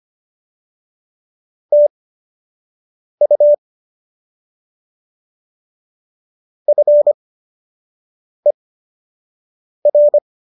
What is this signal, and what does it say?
Morse code, heard as a keyed tone, about 25 wpm, spaced as for 6 wpm, 590 Hz. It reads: TU FER